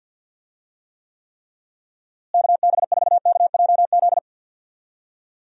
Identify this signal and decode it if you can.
Morse code, heard as a keyed tone, about 45 wpm, 700 Hz. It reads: KB4KJZ
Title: USA Call Signs - 6 Characters - Repeated 3x - 50wpm